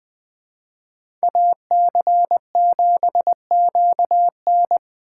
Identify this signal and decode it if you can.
Morse code, heard as a keyed tone, about 24 wpm, 700 Hz. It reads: AC7QN
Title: Speed Racing - USA 2x2 Call Signs 20wpm